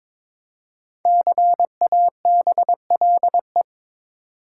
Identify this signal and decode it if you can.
Morse code, heard as a keyed tone, about 21 wpm, 700 Hz. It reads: CABLE
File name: 5 Letter Words - First 3 Letters Given - 22wpm